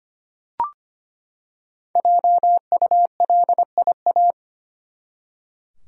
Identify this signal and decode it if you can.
Morse code, heard as a keyed tone, about 25 wpm, 700 Hz. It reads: JULIA
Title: USA Common Names - 25wpm